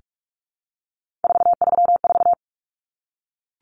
Morse code, heard as a keyed tone, about 44 wpm, 700 Hz. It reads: 434
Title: Common Cut Numbers - 3-Digit 45wpm